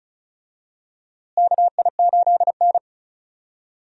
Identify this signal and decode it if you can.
Morse code, heard as a keyed tone, about 36 wpm, 700 Hz. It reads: KI8N
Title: USA Call Signs - 4 Characters - Repeated 2x - 35wpm